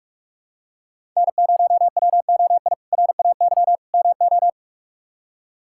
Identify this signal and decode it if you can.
Morse code, heard as a keyed tone, about 46 wpm, 700 Hz. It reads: N0WOI RAY MO
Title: K1USN SST QSO Practice - 45wpm